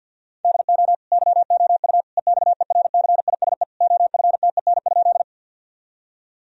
Morse code, heard as a keyed tone, about 50 wpm, 700 Hz. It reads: DO YOU EXERCISE OFTEN?
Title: Sentences from Top 1000 Words - Review Part 4 - 50wpm